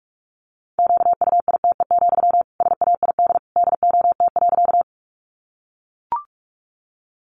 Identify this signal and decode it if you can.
Morse code, heard as a keyed tone, about 44 wpm, 700 Hz. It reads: QUITE, SAID DOT.
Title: Sentences from Top 1500 Words 45wpm